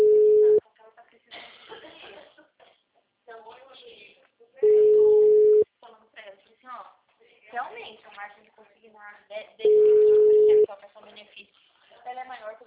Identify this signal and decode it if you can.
Ringback tone (Europe, 425 Hz), with speech.